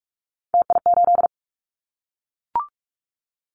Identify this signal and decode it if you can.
Morse code, heard as a keyed tone, about 44 wpm, 700 Hz. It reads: TI8
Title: Callsign Prefixes for Top 100 DXCC - 45wpm